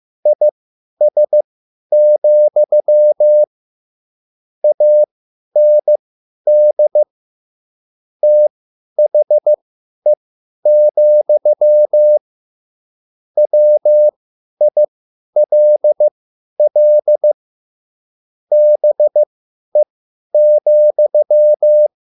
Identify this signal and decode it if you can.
Morse code, heard as a keyed tone, about 15 wpm, spaced as for 10 wpm, 600 Hz. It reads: IS, AND THE, WILL BE,